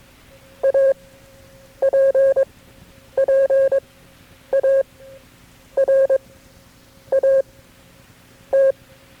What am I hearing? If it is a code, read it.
Morse code, heard as a keyed tone, about 22 wpm, spaced as for 8 wpm, 540 Hz. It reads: APPARAT
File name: FreeFiles / CW-practice / KB6MT-tape3-side1